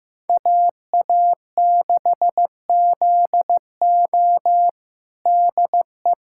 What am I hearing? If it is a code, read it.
Morse code, heard as a keyed tone, about 15 wpm, 700 Hz. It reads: AA6ZO DE
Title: USA Callsign DE Callsign - 5 Characters - 15wpm